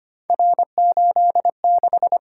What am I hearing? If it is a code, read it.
Morse code, heard as a keyed tone, about 25 wpm, 700 Hz. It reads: R86
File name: Three Letter-Number Combinations - 25wpm